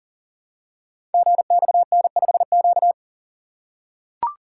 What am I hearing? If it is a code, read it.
Morse code, heard as a keyed tone, about 39 wpm, 700 Hz. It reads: GXN5Q
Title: Code Groups - 5 Letter:Numbers - 40wpm